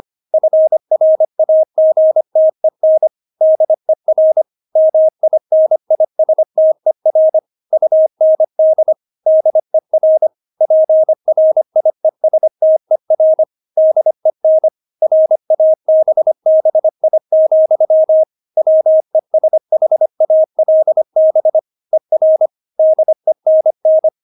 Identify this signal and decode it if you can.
Morse code, heard as a keyed tone, about 25 wpm, 630 Hz. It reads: FRAGTEN DER MINISTER UND DER PRIESTER DEN RABBI, WESHALB ER DENN
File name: Witz6 25wpm